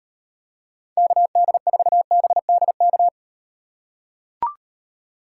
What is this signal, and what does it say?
Morse code, heard as a keyed tone, about 38 wpm, 700 Hz. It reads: KD4BDK